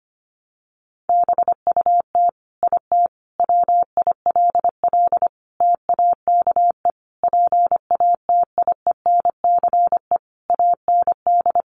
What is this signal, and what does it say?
Morse code, heard as a keyed tone, about 25 wpm, 700 Hz. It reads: BUT IT WILL TAKE PATIENCE AND
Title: Sentences from Top 1300 Words 25wpm